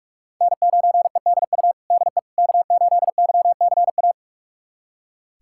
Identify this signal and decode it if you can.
Morse code, heard as a keyed tone, about 42 wpm, 700 Hz. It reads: N9EDU DE K8YCA
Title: USA Callsign DE Callsign - 5 Characters - 45wpm